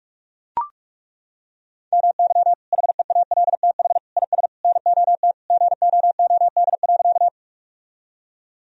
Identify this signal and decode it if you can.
Morse code, heard as a keyed tone, about 43 wpm, 700 Hz. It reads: MY HEALTH IS NOT GOOD.
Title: Sentences from Top 1000 Words - Review Part 3 - 45wpm